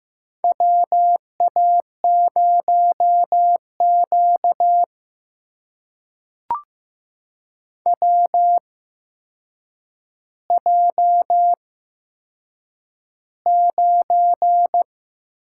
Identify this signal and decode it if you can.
Morse code, heard as a keyed tone, about 15 wpm, spaced as for 7 wpm, 700 Hz. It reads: WA0Q WJ9
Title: USA Call Signs - 4 Characters - 8x Character Spacing - 15wpm